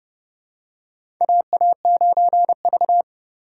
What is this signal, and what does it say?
Morse code, heard as a keyed tone, about 30 wpm, 690 Hz. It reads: AA9V
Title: USA Call Signs - 4 Characters - 2x Character Spacing - 30wpm